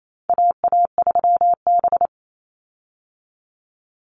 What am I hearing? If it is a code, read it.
Morse code, heard as a keyed tone, about 28 wpm, 700 Hz. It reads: AA3B
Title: Call Signs Repeated 3x - 28wpm